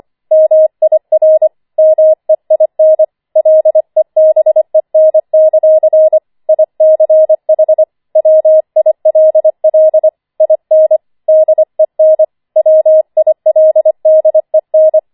Morse code, heard as a keyed tone, about 25 wpm, 630 Hz. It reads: MIR MEIN LEBEN; ICH WILL IN DEN WILDEN
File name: 25wpm